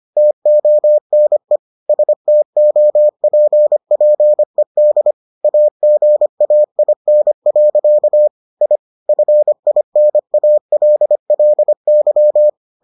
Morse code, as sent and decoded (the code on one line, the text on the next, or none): - --- -. / ... - --- .--. .--. -.. / .- --. .- .. -. .-.-.- / .. / ..-. .. -. .- .-.. .-.. -.--
TONE STOPPED AGAIN. I FINALLY